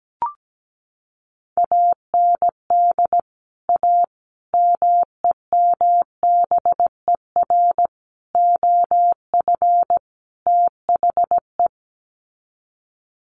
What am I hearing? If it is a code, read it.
Morse code, heard as a keyed tone, about 17 wpm, 700 Hz. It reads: AND A MEMBER OF THE
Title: Sets of 5 Words 17wpm